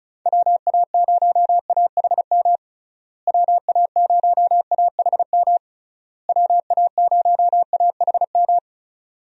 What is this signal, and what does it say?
Morse code, heard as a keyed tone, about 36 wpm, 700 Hz. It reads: WA0AHM WA0AHM WA0AHM
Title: USA Call Signs - 6 Characters - Repeated 3x - 35wpm